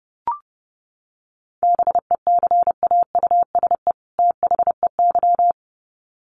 Morse code, heard as a keyed tone, about 30 wpm, 700 Hz. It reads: BECAUSE THEY
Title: Sets of 2 Words 30wpm